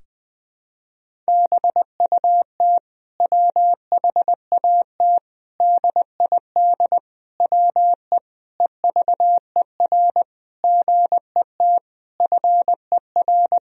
Morse code, heard as a keyed tone, about 21 wpm, 700 Hz. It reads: BUT WHAT DID WE EVER GET FER
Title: Sentences from Top 400 Words and Abbreviations 20wpm